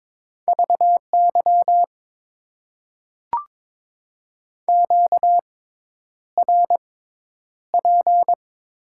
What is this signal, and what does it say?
Morse code, heard as a keyed tone, about 22 wpm, spaced as for 10 wpm, 700 Hz. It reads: VY QRP
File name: Top 90 QSO Elements - 6x Character Spacing - 22wpm